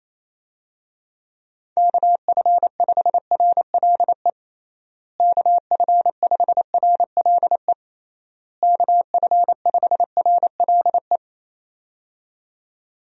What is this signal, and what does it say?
Morse code, heard as a keyed tone, about 27 wpm, 700 Hz. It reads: KF5RLE KF5RLE KF5RLE
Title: USA Call Signs - 6 Characters - Repeated 3x - 28wpm